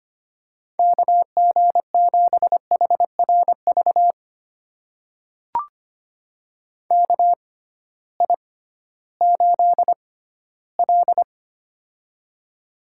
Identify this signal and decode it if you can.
Morse code, heard as a keyed tone, about 25 wpm, spaced as for 15 wpm, 700 Hz. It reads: KG7HRV KI8L
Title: USA Call Signs - 6x Character Spacing - 25wpm